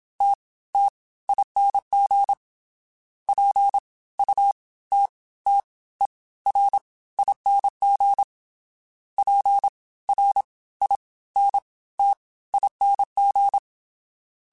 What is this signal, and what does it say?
Morse code, heard as a keyed tone, about 27 wpm, spaced as for 17 wpm, 800 Hz. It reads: TTING PUTTERING PRINTING